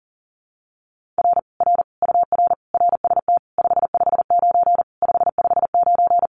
Morse code, heard as a keyed tone, about 40 wpm, 700 Hz. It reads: R R UR RST 559 559
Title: Common QSO Phrases Encore 40wpm